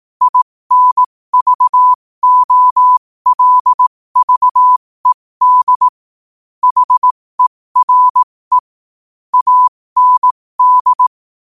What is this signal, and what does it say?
Morse code, heard as a keyed tone, about 18 wpm, spaced as for 15 wpm, 1000 Hz. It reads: INVOLVED HERE AND
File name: qotdfiles / 15wpm